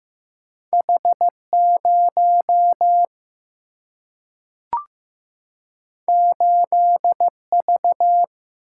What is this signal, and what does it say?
Morse code, heard as a keyed tone, about 15 wpm, 700 Hz. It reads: H 0 8 V
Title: Single-Letter Single-Number 15wpm